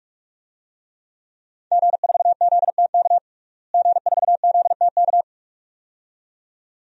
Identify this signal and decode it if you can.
Morse code, heard as a keyed tone, about 46 wpm, 700 Hz. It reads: G4ZTK G4ZTK
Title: DXCC-100 Call Signs - 5 Characters - Repeated 2x - 45wpm